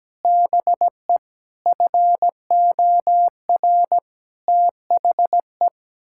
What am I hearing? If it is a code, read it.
Morse code, heard as a keyed tone, about 17 wpm, 700 Hz. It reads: BE FOR THE
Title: Sentences from Top 1000 Words - Review Part 6 - 17wpm